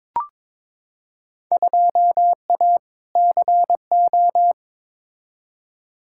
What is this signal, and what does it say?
Morse code, heard as a keyed tone, about 22 wpm, 700 Hz. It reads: 2A CO
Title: Field Day Exchanges - 22wpm